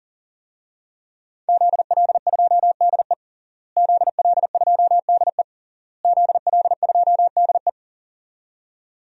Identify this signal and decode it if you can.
Morse code, heard as a keyed tone, about 40 wpm, 700 Hz. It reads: ZL2DE ZL2DE ZL2DE
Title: DXCC-100 Call Signs - 5 Characters - Repeated 3x - 40wpm